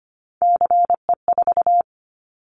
Morse code, heard as a keyed tone, about 25 wpm, 700 Hz. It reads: CE4
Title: Callsign Prefixes for Top 100 DXCC - 25wpm